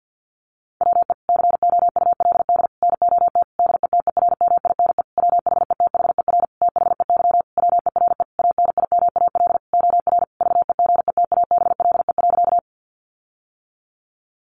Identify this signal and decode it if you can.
Morse code, heard as a keyed tone, about 48 wpm, 700 Hz. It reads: WE COULD NOT DETERMINE WHETHER THEY WERE ANIMAL OR VEGETABLE.
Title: Sentences from Top 1300 Words 50wpm